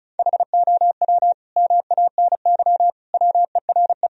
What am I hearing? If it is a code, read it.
Morse code, heard as a keyed tone, about 34 wpm, 700 Hz. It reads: HOW MANY WERE
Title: Sentences from Top 1000 Words - Review Part 1 - 35wpm